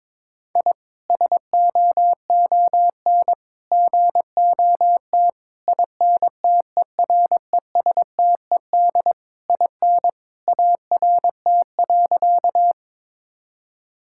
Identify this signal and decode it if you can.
Morse code, heard as a keyed tone, about 22 wpm, 700 Hz. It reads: I SOON GOT INTERESTED IN ART.
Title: Sentences from Top 1000 Words - Review Part 6 - 22wpm